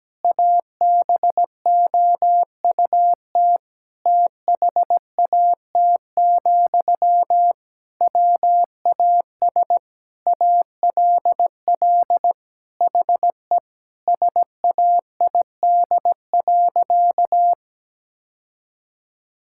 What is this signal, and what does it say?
Morse code, heard as a keyed tone, about 17 wpm, 700 Hz. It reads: ABOUT THAT, WAS ALL HE SAID.
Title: Sentences from Top 1600 Words 17wpm